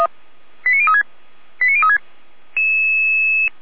Five-tone selective call (ZVEI), heard as quick runs of five tones, tones of about 70 ms. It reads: 89036 89036